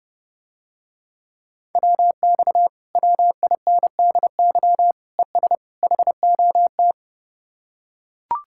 Morse code, heard as a keyed tone, about 32 wpm, 700 Hz. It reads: WX WINDY ES HOT